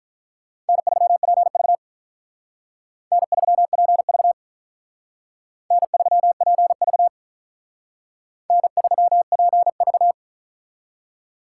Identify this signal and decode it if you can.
Morse code, heard as a keyed tone, about 43 wpm, 700 Hz. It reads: N3PV N3PV N3PV N3PV
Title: Speed Racing - USA Call Signs 35wpm